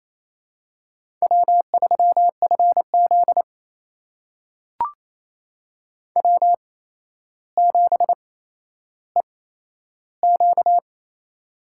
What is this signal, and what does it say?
Morse code, heard as a keyed tone, about 29 wpm, spaced as for 12 wpm, 700 Hz. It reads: W3FZ W7EQ
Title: USA Call Signs - 4 Characters - 8x Character Spacing - 28wpm